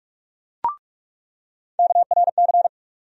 Morse code, heard as a keyed tone, about 47 wpm, 700 Hz. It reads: KRC
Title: USA Call Sign Postfixes - 3 Letters 45wpm